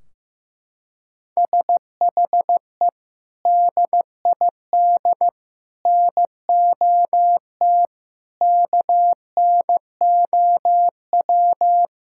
Morse code, heard as a keyed tone, about 15 wpm, 700 Hz. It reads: SHE DID NOT KNOW
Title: Sentences from Top 1600 Words 15wpm